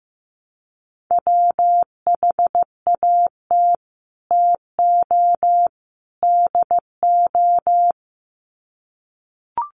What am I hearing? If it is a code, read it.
Morse code, heard as a keyed tone, about 15 wpm, 700 Hz. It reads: WHAT TO DO
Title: Sets of 3 Words 15wpm